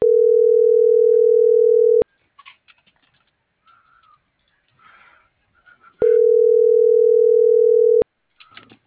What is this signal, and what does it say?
Ringback tone (North America), 440 + 480 Hz.